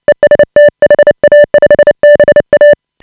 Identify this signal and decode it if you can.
Morse code, heard as a keyed tone, about 29 wpm, 590 Hz. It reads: EST HA5BA
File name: AUDIO2018 / HA5BA CW